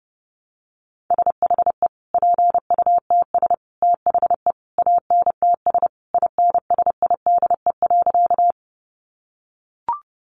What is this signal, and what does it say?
Morse code, heard as a keyed tone, about 30 wpm, 700 Hz. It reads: SHE PUTS THE ANTS INSIDE.
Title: Additional Easy Phrases 30wpm